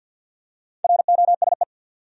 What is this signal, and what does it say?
Morse code, heard as a keyed tone, about 52 wpm, 700 Hz. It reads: ROSE
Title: USA Common Names - Max 4 Letters - 50wpm